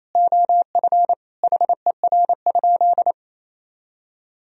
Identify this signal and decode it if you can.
Morse code, heard as a keyed tone, about 28 wpm, 700 Hz. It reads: OF HER?
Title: Sentences from Top 200 Words and Abbreviations 28wpm